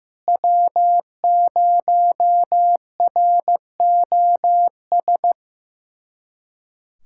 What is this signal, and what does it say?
Morse code, heard as a keyed tone, about 16 wpm, 700 Hz. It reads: W0ROS
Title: USA Callsign DE Callsign - 5 Characters - 15wpm